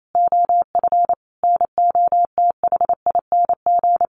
Morse code, heard as a keyed tone, about 28 wpm, 700 Hz. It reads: OF NOTHING
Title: Sentences from Top 500 Words 28wpm